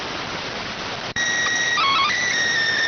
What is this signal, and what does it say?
SSTV (slow-scan television), PD 290.